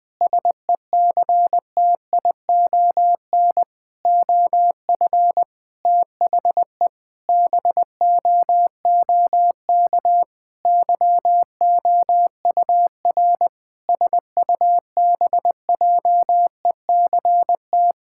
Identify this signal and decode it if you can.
Morse code, heard as a keyed tone, about 20 wpm, 700 Hz. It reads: SECTION OF THE BOOK YOUR SUBJECT
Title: Sentences from Top 1000 Words - Review Part 5 - 20wpm